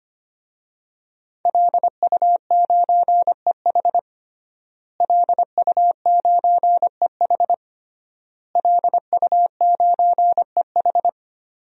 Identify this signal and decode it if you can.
Morse code, heard as a keyed tone, about 25 wpm, 700 Hz. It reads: LU9EH LU9EH LU9EH
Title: DXCC-100 Call Signs - 5 Characters - Repeated 3x - 25wpm